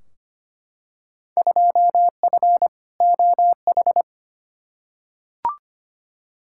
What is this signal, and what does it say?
Morse code, heard as a keyed tone, about 25 wpm, 700 Hz. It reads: 2F OH
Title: Field Day Exchanges - 25wpm